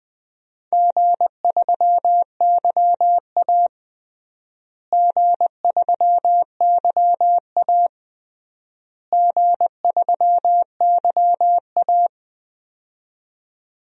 Morse code, heard as a keyed tone, about 20 wpm, 700 Hz. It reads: G3YA G3YA G3YA
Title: DXCC-100 Call Signs - 4 Characters - Repeated 3x - 20wpm